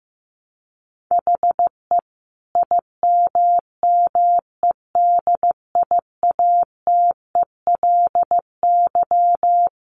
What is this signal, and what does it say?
Morse code, heard as a keyed tone, about 15 wpm, 700 Hz. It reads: HE IMMEDIATELY
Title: Sentences from Top 1100 Words 15wpm